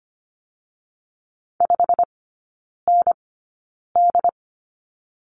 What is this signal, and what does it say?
Morse code, heard as a keyed tone, about 25 wpm, 700 Hz. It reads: 5 N D
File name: Lesson 17e - Letter u - Three Characters 25wpm